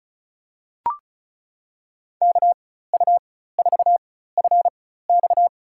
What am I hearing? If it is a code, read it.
Morse code, heard as a keyed tone, about 36 wpm, spaced as for 16 wpm, 700 Hz. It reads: KU4FX
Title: USA Call Signs - 4x Character Spacing - 35wpm